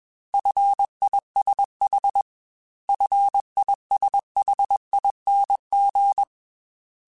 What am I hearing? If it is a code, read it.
Morse code, heard as a keyed tone, about 21 wpm, 800 Hz. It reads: FISH FISHING
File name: Fs / Single F mp3s / Single F Prefixes 3x2 Twice @20